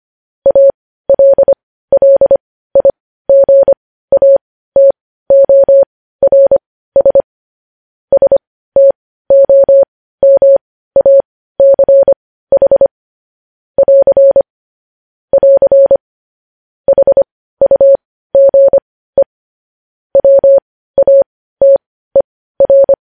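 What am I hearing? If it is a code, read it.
Morse code, heard as a keyed tone, about 25 wpm, spaced as for 15 wpm, 550 Hz. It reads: ALLIGATORS STOMACH <AR> <AR> HUGE WATER
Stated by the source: Morse code news headlines at 15 wpm.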